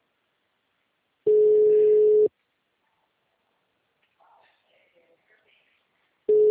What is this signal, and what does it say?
Ringback tone (Europe, 425 Hz), with speech.